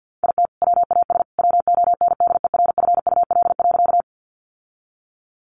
Morse code, heard as a keyed tone, about 50 wpm, 700 Hz. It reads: IT WAS WONDERFUL.
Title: Sentences from Top 1100 Words 50wpm